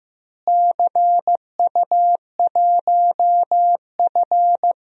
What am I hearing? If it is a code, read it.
Morse code, heard as a keyed tone, about 15 wpm, 700 Hz. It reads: CU1F
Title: DXCC-100 Call Signs - 4 Characters - Repeated 2x - 15wpm